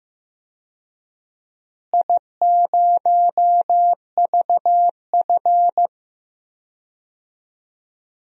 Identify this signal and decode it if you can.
Morse code, heard as a keyed tone, about 15 wpm, 700 Hz. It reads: I0VF
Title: DXCC-100 Call Signs - 4 Characters - Repeated 3x - 15wpm